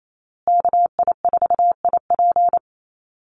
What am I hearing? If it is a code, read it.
Morse code, heard as a keyed tone, about 28 wpm, 700 Hz. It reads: KI4IP
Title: USA Call Signs Repeated 3x 28wpm